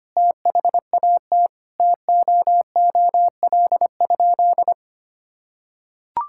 Morse code, heard as a keyed tone, about 25 wpm, 700 Hz. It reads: THAT TOOL?
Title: Sentences from Top 1000 Words - Review Part 6 - 25wpm